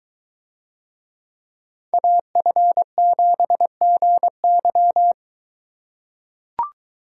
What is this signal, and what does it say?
Morse code, heard as a keyed tone, about 21 wpm, 700 Hz. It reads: AF7GY